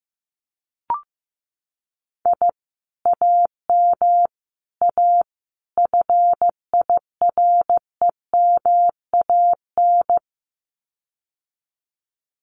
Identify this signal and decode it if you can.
Morse code, heard as a keyed tone, about 15 wpm, 700 Hz. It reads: I AM A FIREMAN
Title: Common QSO Phrases Encore 15wpm